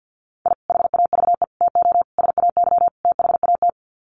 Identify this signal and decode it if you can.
Morse code, heard as a keyed tone, about 50 wpm, 700 Hz. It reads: I HAVE TO SAY THAT
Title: Sets of 5 Words Encore 50wpm